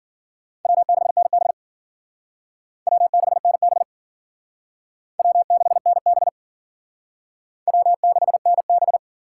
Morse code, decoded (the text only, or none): W6NB W6NB W6NB W6NB